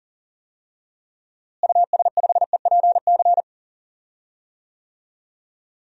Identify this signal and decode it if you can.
Morse code, heard as a keyed tone, about 40 wpm, 700 Hz. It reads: US5EPC
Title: DXCC-100 Call Signs - 6 Characters - Repeated 3x - 40wpm